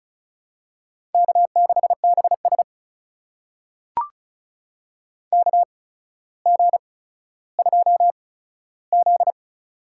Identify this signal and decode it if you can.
Morse code, heard as a keyed tone, about 34 wpm, spaced as for 15 wpm, 700 Hz. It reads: K6BS KG2Z
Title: USA Call Signs - 4 Characters - 8x Character Spacing - 35wpm